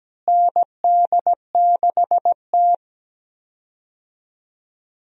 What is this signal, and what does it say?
Morse code, heard as a keyed tone, about 17 wpm, 700 Hz. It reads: ND6T
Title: USA Call Signs - 4 Characters - Repeated 2x - 17wpm